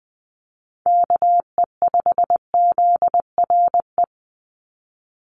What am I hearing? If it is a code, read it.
Morse code, heard as a keyed tone, about 20 wpm, 700 Hz. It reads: KE5ZRE
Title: USA Call Signs 20wpm